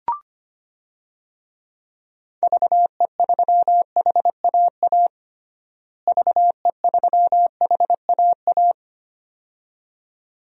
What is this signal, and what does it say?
Morse code, heard as a keyed tone, about 25 wpm, 700 Hz. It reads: VE3HAA VE3HAA
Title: DXCC-100 Call Signs - 6 Characters - Repeated 2x - 25wpm